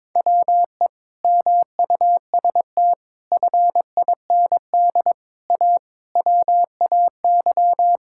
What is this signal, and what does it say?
Morse code, heard as a keyed tone, about 21 wpm, 700 Hz. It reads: WE MUST FIND A WAY
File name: practice / 22wpm / Sentences with 4-Word Phrases - 22wpm